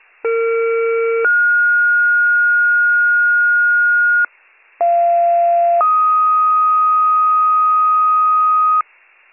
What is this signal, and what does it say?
Two-tone page, heard as a one-second tone, then a three-second tone, about 460 then 1430 Hz, about 690 then 1250 Hz, with noise. It is heard from 0.2 to 8.8 s.